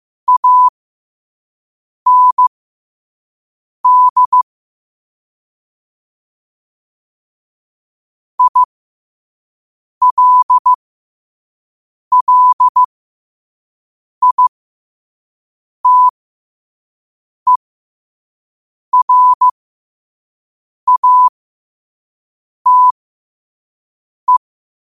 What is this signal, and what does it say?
Morse code, heard as a keyed tone, about 15 wpm, spaced as for 5 wpm, 1000 Hz. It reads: AND ILLITERATE
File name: qotdfiles / 5wpm